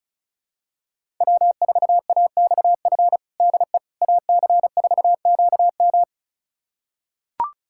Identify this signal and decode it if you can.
Morse code, heard as a keyed tone, about 36 wpm, 700 Hz. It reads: W4AXF DE AC4QM
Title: USA Callsign DE Callsign - 5 Characters - 35wpm